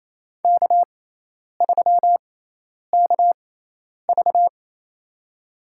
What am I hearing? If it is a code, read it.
Morse code, heard as a keyed tone, about 28 wpm, spaced as for 10 wpm, 700 Hz. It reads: K3KV
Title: USA Call Signs - 4 Characters - 6x Character Spacing - 28wpm